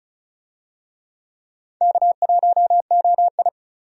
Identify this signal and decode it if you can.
Morse code, heard as a keyed tone, about 36 wpm, 700 Hz. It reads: K1OI
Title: USA Call Signs - 4 Characters - 35wpm